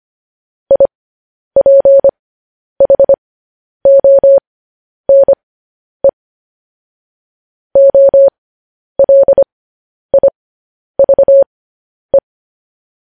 Morse code, heard as a keyed tone, about 25 wpm, spaced as for 10 wpm, 550 Hz. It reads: IPHONE OLIVE
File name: podcast / News headlines-1x-25x10